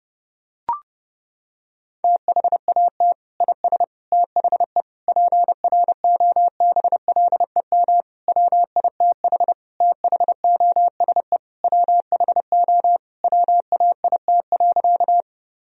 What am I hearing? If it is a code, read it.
Morse code, heard as a keyed tone, about 30 wpm, 700 Hz. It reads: THAT IS THE PROBLEM WITH THOSE WHO WAIT.
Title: Sentences from Top 1000 Words - Review Part 2 - 30wpm